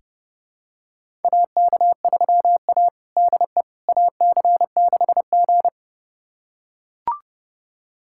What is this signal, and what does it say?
Morse code, heard as a keyed tone, about 30 wpm, 700 Hz. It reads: AK3A DE AC6G
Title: USA Callsign DE Callsign - 4 Characters - 30wpm